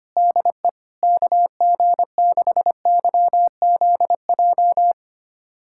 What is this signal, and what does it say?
Morse code, heard as a keyed tone, about 25 wpm, 700 Hz. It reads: DE KG6YZJ
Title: USA Callsign DE Callsign - 6 Characters - 25wpm